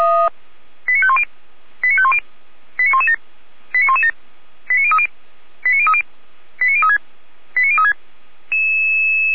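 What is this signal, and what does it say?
Five-tone selective call (ZVEI), heard as quick runs of five tones, tones of about 70 ms. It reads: 89510 89510 89117 89117 89030 89030 89036 89036
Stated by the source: ZVEI five-tone selective call received on SDR.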